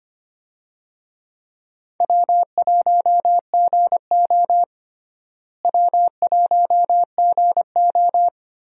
Morse code, heard as a keyed tone, about 25 wpm, 700 Hz. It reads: W1GO W1GO
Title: USA Call Signs - 4 Characters - Repeated 3x - 25wpm